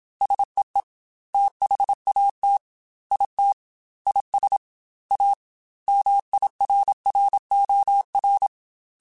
Morse code, heard as a keyed tone, about 27 wpm, 800 Hz. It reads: SEE THAT IT IS A MIRROR
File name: Technical Subjects / Technical Subjects-mp3s / OR Gate @25